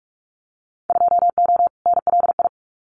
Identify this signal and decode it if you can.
Morse code, heard as a keyed tone, about 44 wpm, 700 Hz. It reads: 2O NLI
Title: Winter Field Day Exchanges 45wpm